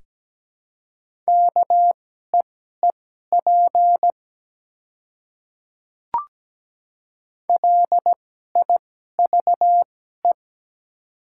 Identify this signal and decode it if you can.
Morse code, heard as a keyed tone, about 17 wpm, spaced as for 12 wpm, 690 Hz. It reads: KEEP LIVE
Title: 2x Character Spacing - Top 300 Words - 17wpm